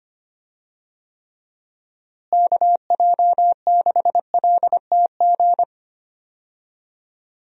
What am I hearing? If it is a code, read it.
Morse code, heard as a keyed tone, about 25 wpm, 700 Hz. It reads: KJ6LTG
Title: USA Call Signs - 6 Characters - Repeated 2x - 25wpm